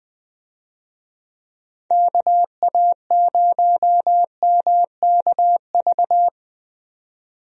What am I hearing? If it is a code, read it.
Morse code, heard as a keyed tone, about 21 wpm, 700 Hz. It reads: KA0MKV
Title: USA Call Signs - 6 Characters - 20wpm